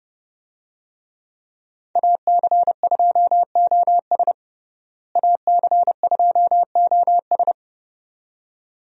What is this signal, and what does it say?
Morse code, heard as a keyed tone, about 30 wpm, 700 Hz. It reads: AC2OS AC2OS
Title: USA Call Signs - 5 Characters - Repeated 2x - 30wpm